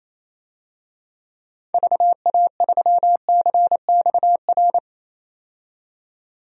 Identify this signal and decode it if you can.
Morse code, heard as a keyed tone, about 28 wpm, 700 Hz. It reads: VA3CXR
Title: DXCC-100 Call Signs - 6 Characters - 28wpm